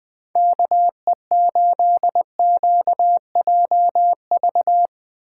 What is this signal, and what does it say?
Morse code, heard as a keyed tone, about 20 wpm, 700 Hz. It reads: KE8QJV
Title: 2023 POTA Activator Call Signs - 20wpm